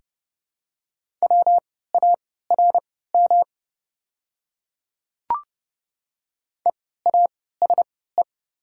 Morse code, heard as a keyed tone, about 30 wpm, spaced as for 17 wpm, 700 Hz. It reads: WARM EASE
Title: 3x Character Spacing - Top 500 Words - 30wpm